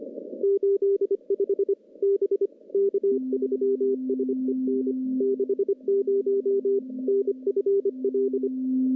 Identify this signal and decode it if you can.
Morse code, heard as a keyed tone, about 25 wpm, 400 Hz. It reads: 85 B K3SEN 60 NFL
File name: contests / CW-SS-k5tr- -1200